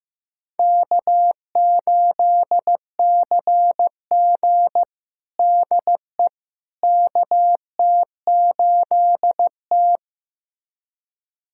Morse code, heard as a keyed tone, about 15 wpm, 700 Hz. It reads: K8CG DE KT8T
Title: USA Callsign DE Callsign - 4 Characters - 15wpm